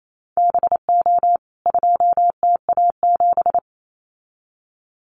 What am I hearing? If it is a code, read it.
Morse code, heard as a keyed tone, about 28 wpm, 700 Hz. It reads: BO 2TA7
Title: CWT Exchange 28wpm